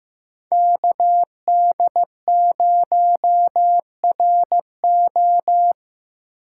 Morse code, heard as a keyed tone, about 15 wpm, 700 Hz. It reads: KD0RO
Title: USA Call Signs - 5 Characters - 3x Character Spacing - 15wpm